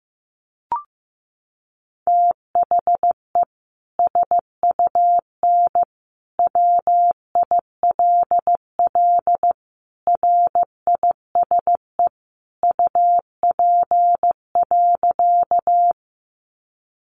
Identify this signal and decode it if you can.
Morse code, heard as a keyed tone, about 15 wpm, 700 Hz. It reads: THE SUN WILL RISE UP.
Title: Sentences from Top 900 Words 15wpm